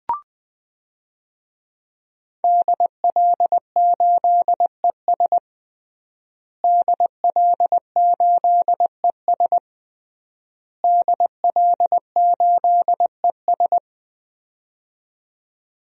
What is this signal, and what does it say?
Morse code, heard as a keyed tone, about 20 wpm, 700 Hz. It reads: DL8ES DL8ES DL8ES
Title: DXCC-100 Call Signs - 5 Characters - Repeated 3x - 20wpm